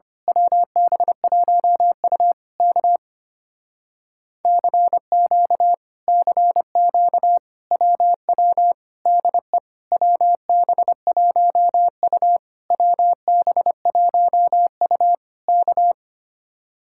Morse code, heard as a keyed tone, about 24 wpm, 700 Hz. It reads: WB1U K CQ CQ WW DE WB1U WB1U K
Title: Speed-Racing Common QSO Phrases - 25wpm